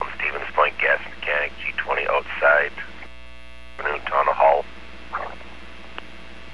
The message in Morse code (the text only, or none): none